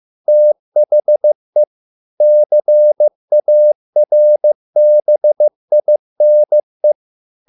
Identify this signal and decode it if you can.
Morse code, heard as a keyed tone, about 15 wpm, 600 Hz. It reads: THE CARBINE